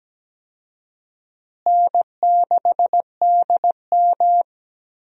Morse code, heard as a keyed tone, about 17 wpm, 700 Hz. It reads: N6DM
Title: USA Call Signs - 4 Characters - Repeated 3x - 17wpm